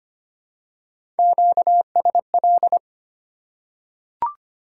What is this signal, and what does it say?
Morse code, heard as a keyed tone, about 25 wpm, 700 Hz. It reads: QSL
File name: Top 90 QSO Elements - 2x Character Spacing - 25wpm